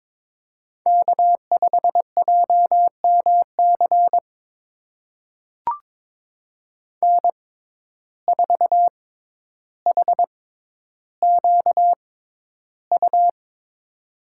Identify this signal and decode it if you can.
Morse code, heard as a keyed tone, about 22 wpm, spaced as for 11 wpm, 700 Hz. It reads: K5JMC N4HQU
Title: USA Call Signs - 5 Characters - 6x Character Spacing - 22wpm